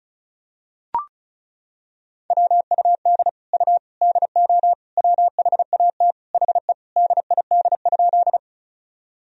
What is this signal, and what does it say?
Morse code, heard as a keyed tone, about 36 wpm, 700 Hz. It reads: WUD U DO WHAT HE DID?
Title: Sentences from Top 100 Words and Abbreviations 35wpm